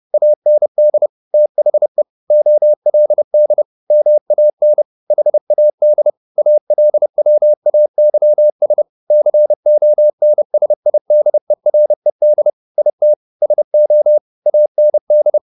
Morse code, heard as a keyed tone, about 30 wpm, 600 Hz. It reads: AND THE OLD MAN HAD ALWAYS CONSIDERED IT SO AND
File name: the old man and the sea / the old man and the sea 1 30wpm